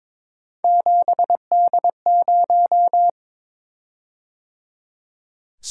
Morse code, heard as a keyed tone, about 22 wpm, 700 Hz. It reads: 7D0
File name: practice / 22wpm / Three Letter-Number Combinations - 22wpm